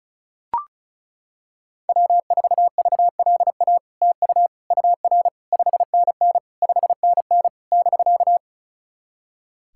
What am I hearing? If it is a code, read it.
Morse code, heard as a keyed tone, about 36 wpm, 700 Hz. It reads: W4VLA TU UR 5NN 5NN <BK>
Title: SOTA QSO Practice 35wpm